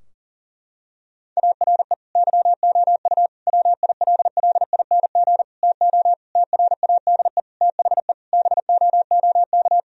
Morse code, heard as a keyed tone, about 40 wpm, 700 Hz. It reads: ARE YOU WILLING TO TRADE THE BOOK
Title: Sentences from Top 1000 Words - Review Part 4 - 40wpm